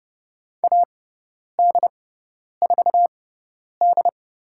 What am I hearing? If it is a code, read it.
Morse code, heard as a keyed tone, about 30 wpm, spaced as for 10 wpm, 700 Hz. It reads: AD4D
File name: Lesson 17g - Letter u - Call Signs 30wpm